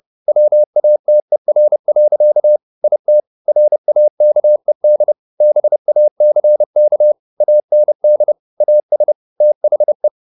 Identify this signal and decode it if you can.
Morse code, heard as a keyed tone, about 30 wpm, 600 Hz. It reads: WATER. IT RAKED BACK AND AS THE